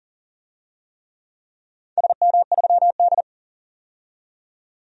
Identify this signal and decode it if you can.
Morse code, heard as a keyed tone, about 42 wpm, 700 Hz. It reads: SM3D
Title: DXCC-100 Call Signs - 4 Characters - 40wpm